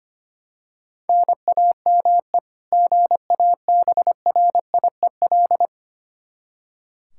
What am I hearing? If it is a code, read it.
Morse code, heard as a keyed tone, about 25 wpm, 700 Hz. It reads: NAME GABRIEL
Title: K1USN SST QSO Practice 25wpm